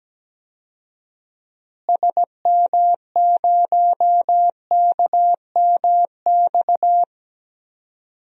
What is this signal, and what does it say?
Morse code, heard as a keyed tone, about 17 wpm, 700 Hz. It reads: SM0KMX
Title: DXCC-100 Call Signs - 6 Characters - Repeated 2x - 17wpm